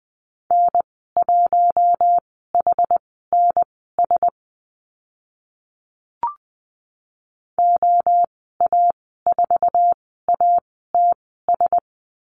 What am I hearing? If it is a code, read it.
Morse code, heard as a keyed tone, about 20 wpm, spaced as for 14 wpm, 700 Hz. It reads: N1HNS OA4ATS
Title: 076 Lesson 13g - Letter L - Call Signs 20wpm